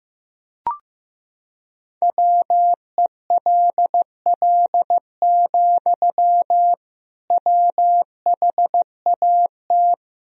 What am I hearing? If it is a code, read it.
Morse code, heard as a keyed tone, about 16 wpm, 700 Hz. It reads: WELL, WHAT
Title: Sentences from Top 200 Words and Abbreviations 15wpm